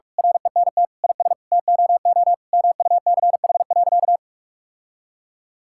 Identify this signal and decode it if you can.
Morse code, heard as a keyed tone, about 41 wpm, 700 Hz. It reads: RENT IS TOO MUCH.
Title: Sentences from Top 1000 Words - Review Part 5 - 45wpm